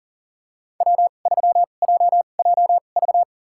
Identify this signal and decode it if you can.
Morse code, heard as a keyed tone, about 40 wpm, spaced as for 29 wpm, 700 Hz. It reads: W3JJV
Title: USA Call Signs - 5 Characters - 2x Character Spacing - 40wpm